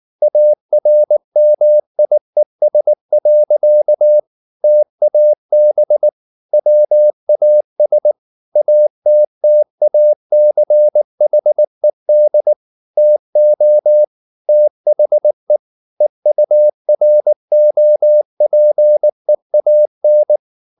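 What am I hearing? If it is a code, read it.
Morse code, heard as a keyed tone, about 19 wpm, 600 Hz. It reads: ARMIES. TAB WAS ATTACHED TO THE EUROPEAN
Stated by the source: Morse code at 19 wpm from CW med Gnister, reading Ken Follett's novel Never in English.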